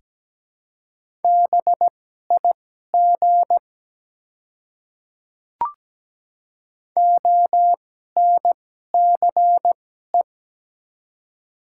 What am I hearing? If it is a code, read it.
Morse code, heard as a keyed tone, about 17 wpm, spaced as for 12 wpm, 700 Hz. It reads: BIG ONCE
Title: 2x Character Spacing - Top 300 Words - 17wpm